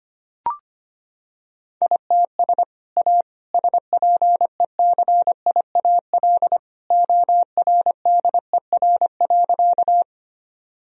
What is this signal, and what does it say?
Morse code, heard as a keyed tone, about 25 wpm, 700 Hz. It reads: ITS A SPECIAL ORDER.
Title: Sentences from Top 1000 Words - Review Part 3 - 25wpm